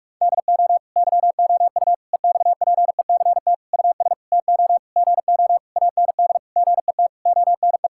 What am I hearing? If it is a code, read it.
Morse code, heard as a keyed tone, about 46 wpm, 700 Hz. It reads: DO YOU EXPECT US TO GO AND GET ONE
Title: Sentences from Top 1600 Words 45wpm